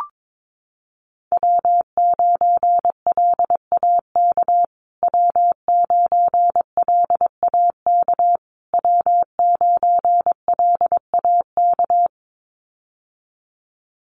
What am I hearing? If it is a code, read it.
Morse code, heard as a keyed tone, about 22 wpm, 700 Hz. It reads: W9LAK W9LAK W9LAK
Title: USA Call Signs Repeated 3x 22wpm